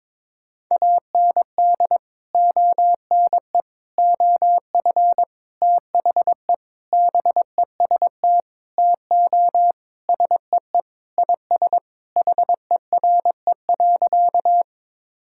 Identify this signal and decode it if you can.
Morse code, heard as a keyed tone, about 22 wpm, 700 Hz. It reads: AND ONE OF THE BEST TO SEE IS HERE.
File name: practice / 22wpm / Sentences with 4-Word Phrases - 22wpm